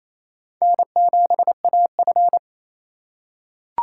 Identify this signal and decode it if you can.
Morse code, heard as a keyed tone, about 28 wpm, 700 Hz. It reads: N7AF
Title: USA Call Signs - 4 Characters - 6x Character Spacing - 28wpm